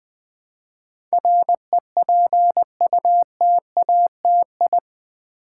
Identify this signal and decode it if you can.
Morse code, heard as a keyed tone, about 20 wpm, 700 Hz. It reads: REPUTATI
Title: Build A Word - 4 Letters At A Time - 10-Letter Words - 20wpm